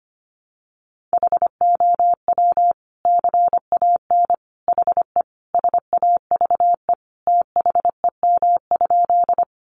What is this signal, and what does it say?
Morse code, heard as a keyed tone, about 25 wpm, 700 Hz. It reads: HOW CAN HE SAVE THEM?
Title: Sentences from Top 1100 Words 25wpm